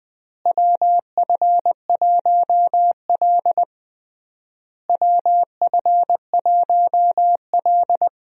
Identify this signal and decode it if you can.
Morse code, heard as a keyed tone, about 20 wpm, 700 Hz. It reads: WF1L WF1L
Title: USA Call Signs - 4 Characters - Repeated 3x - 20wpm